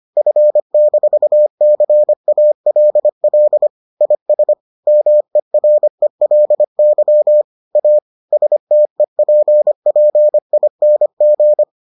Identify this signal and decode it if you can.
Morse code, heard as a keyed tone, about 25 wpm, 600 Hz. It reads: F-CALL IS MERELY A STEPPING